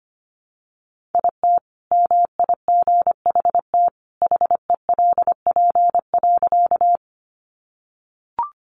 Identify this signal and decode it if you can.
Morse code, heard as a keyed tone, about 25 wpm, 700 Hz. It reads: IT MIGHT HELP.
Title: Sentences from Top 300 Words 25wpm